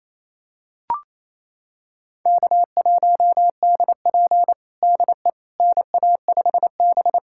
Morse code, heard as a keyed tone, about 28 wpm, 700 Hz. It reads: K1DP DE NA5B
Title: USA Callsign DE Callsign - 4 Characters - 28wpm